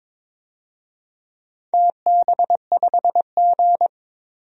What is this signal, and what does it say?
Morse code, heard as a keyed tone, about 22 wpm, 700 Hz. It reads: TB5G
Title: DXCC-100 Call Signs - 4 Characters - 22wpm